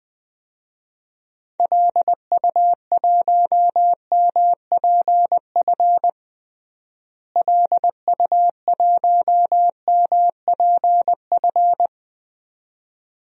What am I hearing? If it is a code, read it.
Morse code, heard as a keyed tone, about 20 wpm, 700 Hz. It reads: LU1MPF LU1MPF
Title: DXCC-100 Call Signs - 6 Characters - Repeated 2x - 20wpm